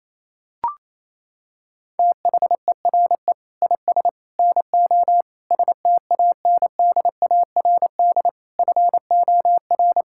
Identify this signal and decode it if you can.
Morse code, heard as a keyed tone, about 27 wpm, 700 Hz. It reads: THERE IS NO STANDARD FOR
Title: Sentences from Top 1600 Words 28wpm